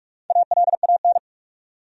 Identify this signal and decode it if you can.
Morse code, heard as a keyed tone, about 46 wpm, 700 Hz. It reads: ALAN
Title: USA Common Names - 45wpm